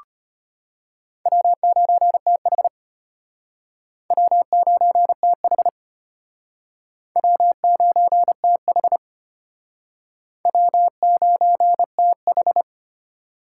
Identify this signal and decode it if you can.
Morse code, heard as a keyed tone, about 33 wpm, 700 Hz. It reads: W9TH W9TH W9TH W9TH